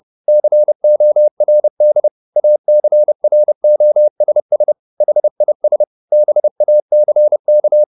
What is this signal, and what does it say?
Morse code, heard as a keyed tone, about 30 wpm, 600 Hz. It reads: CORD ACROSS HIS BACK